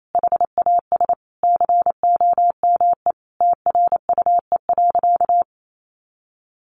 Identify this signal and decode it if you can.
Morse code, heard as a keyed tone, about 28 wpm, 700 Hz. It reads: HAS COME TRUE.
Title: Sentences from Top 1500 Words 28wpm